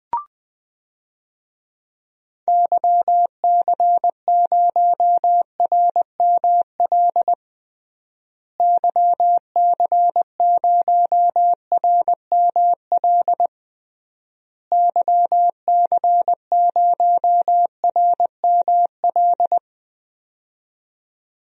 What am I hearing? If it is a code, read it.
Morse code, heard as a keyed tone, about 21 wpm, 700 Hz. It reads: YC0RML YC0RML YC0RML
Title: DXCC-100 Call Signs - 6 Characters - Repeated 3x - 20wpm